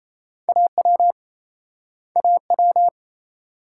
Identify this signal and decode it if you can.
Morse code, heard as a keyed tone, about 39 wpm, 700 Hz. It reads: AW AW